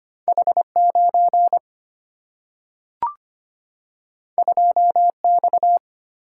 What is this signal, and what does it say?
Morse code, heard as a keyed tone, about 25 wpm, 700 Hz. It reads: H9 2X
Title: Single-Letter Single-Number 25wpm